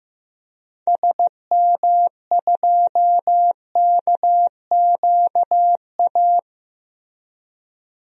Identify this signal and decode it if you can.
Morse code, heard as a keyed tone, about 15 wpm, 700 Hz. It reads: SM2KQA
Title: DXCC-100 Call Signs - 6 Characters - Repeated 3x - 15wpm